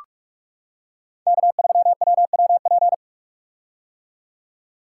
Morse code, heard as a keyed tone, about 46 wpm, 700 Hz. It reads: K3WWP
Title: K1USN SST QSO Practice - 45wpm